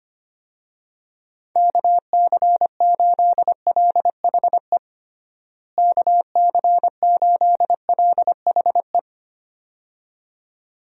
Morse code, heard as a keyed tone, about 25 wpm, 700 Hz. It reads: KC8LHE KC8LHE
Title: USA Call Signs - 6 Characters - Repeated 2x - 25wpm